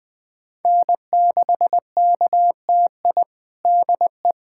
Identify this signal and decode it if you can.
Morse code, heard as a keyed tone, about 21 wpm, 700 Hz. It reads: N6KTI DE
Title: USA Callsign DE Callsign - 5 Characters - 20wpm